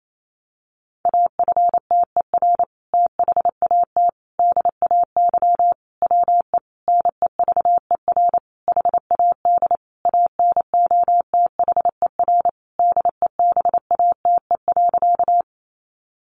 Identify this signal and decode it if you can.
Morse code, heard as a keyed tone, about 28 wpm, 700 Hz. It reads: AFTER THAT DAY WE NEVER HAD ANOTHER DEBATE.
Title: Sentences from Top 1100 Words 28wpm